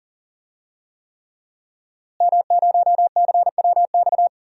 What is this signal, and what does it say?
Morse code, heard as a keyed tone, about 41 wpm, 700 Hz. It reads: M0CWX
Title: DXCC-100 Call Signs - 5 Characters - Repeated 3x - 40wpm